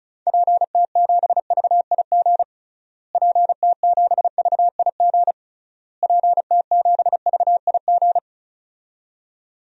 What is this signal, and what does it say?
Morse code, heard as a keyed tone, about 36 wpm, 700 Hz. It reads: PT7VIG PT7VIG PT7VIG
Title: DXCC-100 Call Signs - 6 Characters - Repeated 3x - 35wpm